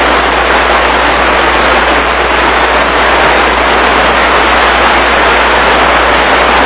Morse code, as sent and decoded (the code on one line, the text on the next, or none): none